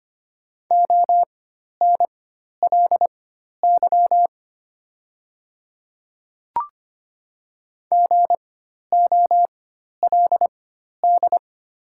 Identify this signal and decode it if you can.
Morse code, heard as a keyed tone, about 25 wpm, spaced as for 12 wpm, 700 Hz. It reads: ONLY GOLD
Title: 4x Character Spacing - Top 500 Words - 25wpm